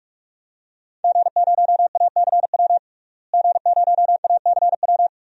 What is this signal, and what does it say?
Morse code, heard as a keyed tone, about 41 wpm, 700 Hz. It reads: G0ACW G0ACW
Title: DXCC-100 Call Signs - 5 Characters - Repeated 3x - 45wpm